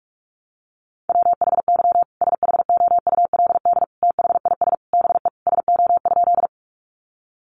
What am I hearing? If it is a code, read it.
Morse code, heard as a keyed tone, about 44 wpm, 700 Hz. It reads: WHY SHOULD THIS BE SO?
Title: Sentences from Top 300 Words 45wpm